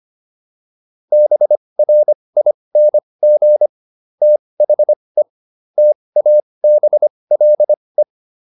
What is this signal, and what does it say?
Morse code, heard as a keyed tone, about 25 wpm, spaced as for 20 wpm, 600 Hz. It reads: BRING THE TABLE